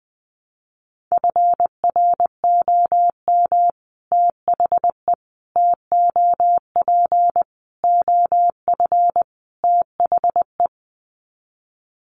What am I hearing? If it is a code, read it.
Morse code, heard as a keyed tone, about 20 wpm, 700 Hz. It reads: FROM THE TOP OF THE
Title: Sets of 5 Words Encore 20wpm